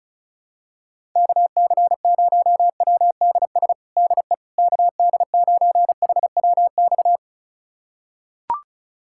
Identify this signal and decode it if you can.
Morse code, heard as a keyed tone, about 36 wpm, 700 Hz. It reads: KC0WDS DE KD9HWX
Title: USA Callsign DE Callsign - 6 Characters - 35wpm